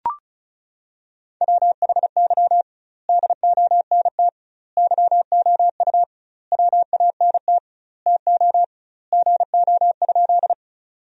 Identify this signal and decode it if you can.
Morse code, heard as a keyed tone, about 36 wpm, 700 Hz. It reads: WHY DONT YOU WANT TO GO?
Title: 2x Word-Spacing - Sentences from Top 300 Words - 35wpm